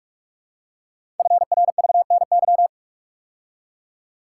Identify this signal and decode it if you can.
Morse code, heard as a keyed tone, about 44 wpm, 700 Hz. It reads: FRVNY
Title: Code Groups - 5 Letters - 45wpm